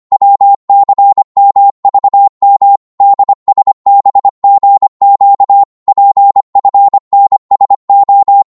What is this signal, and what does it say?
Morse code, heard as a keyed tone, about 25 wpm, 800 Hz. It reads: WCMVM DSBGQ PFNSO
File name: Morse / ltr-25wpm